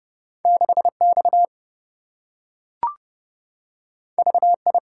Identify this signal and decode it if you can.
Morse code, heard as a keyed tone, about 27 wpm, 700 Hz. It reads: 6X VI